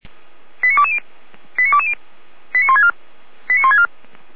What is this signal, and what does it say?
Five-tone selective call (ZVEI), heard as quick runs of five tones, tones of about 70 ms. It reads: 89220 89220 89164 89164
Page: SDR/ZVEI